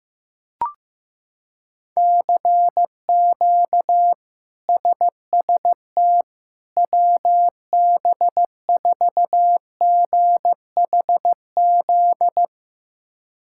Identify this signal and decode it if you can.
Morse code, heard as a keyed tone, about 15 wpm, 700 Hz. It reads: CQ SST WB4GHZ
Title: K1USN SST QSO Practice - 15wpm